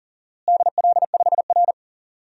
Morse code, heard as a keyed tone, about 40 wpm, 700 Hz. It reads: DL5R
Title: DXCC-100 Call Signs - 4 Characters - Repeated 3x - 40wpm